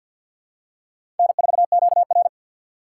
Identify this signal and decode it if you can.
Morse code, heard as a keyed tone, about 48 wpm, 700 Hz. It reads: N4QR
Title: USA Call Signs - 4 Characters - 6x Character Spacing - 50wpm